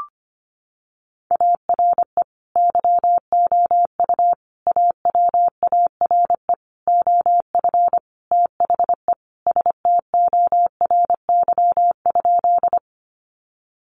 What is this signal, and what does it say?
Morse code, heard as a keyed tone, about 25 wpm, 700 Hz. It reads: ARE YOU AWARE OF THE STORY?
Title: Sentences from Top 1200 Words 25wpm